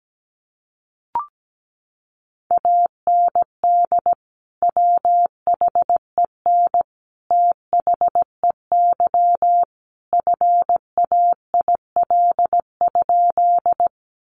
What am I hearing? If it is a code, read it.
Morse code, heard as a keyed tone, about 17 wpm, 700 Hz. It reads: AND WHEN THEY FAIL?
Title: Sentences from Top 1100 Words 17wpm